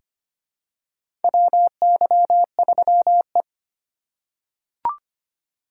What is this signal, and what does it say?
Morse code, heard as a keyed tone, about 25 wpm, 700 Hz. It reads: WY3E
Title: USA Call Signs - 4 Characters - 4x Character Spacing - 25wpm